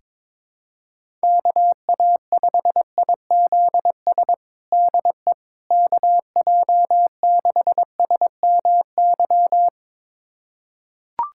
Morse code, heard as a keyed tone, about 22 wpm, 700 Hz. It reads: KA5IZS DE KJ6SMY
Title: USA Callsign DE Callsign - 6 Characters - 22wpm